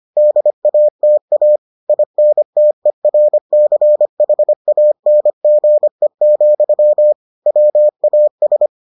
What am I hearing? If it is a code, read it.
Morse code, heard as a keyed tone, about 25 wpm, 600 Hz. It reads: DATA INTERCHANGE, WAS